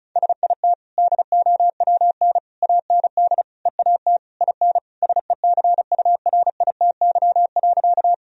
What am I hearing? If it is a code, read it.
Morse code, heard as a keyed tone, about 36 wpm, 700 Hz. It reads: SIT DOWN AND EAT IN SECURITY.
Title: Sentences from Top 1000 Words - Review Part 4 - 35wpm